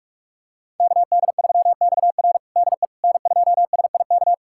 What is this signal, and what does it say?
Morse code, heard as a keyed tone, about 46 wpm, 700 Hz. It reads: KD3XR DE N2SIK
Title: USA Callsign DE Callsign - 5 Characters - 45wpm